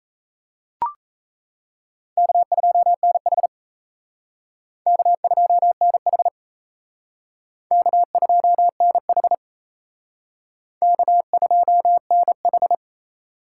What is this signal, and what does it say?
Morse code, heard as a keyed tone, about 34 wpm, 700 Hz. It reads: K2NH K2NH K2NH K2NH
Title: Speed Racing - USA 1x2 Call Signs 28wpm